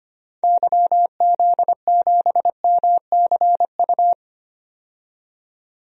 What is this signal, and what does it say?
Morse code, heard as a keyed tone, about 25 wpm, 700 Hz. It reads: YZ7MCU
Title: DXCC-100 Call Signs - 6 Characters - 25wpm